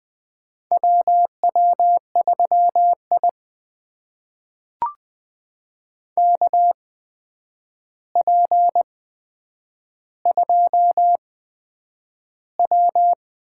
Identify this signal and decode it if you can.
Morse code, heard as a keyed tone, about 20 wpm, spaced as for 9 wpm, 700 Hz. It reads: WW3I KP2W
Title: USA Call Signs - 4 Characters - 8x Character Spacing - 20wpm